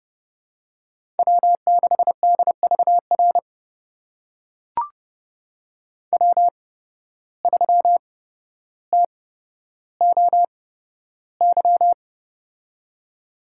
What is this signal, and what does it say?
Morse code, heard as a keyed tone, about 32 wpm, spaced as for 13 wpm, 700 Hz. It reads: W6DVR W3TOY
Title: USA Call Signs - 5 Characters - 8x Character Spacing - 30wpm